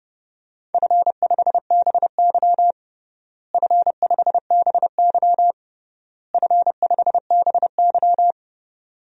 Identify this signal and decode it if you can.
Morse code, heard as a keyed tone, about 30 wpm, 700 Hz. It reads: F5BY F5BY F5BY
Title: DXCC-100 Call Signs - 4 Characters - Repeated 3x - 30wpm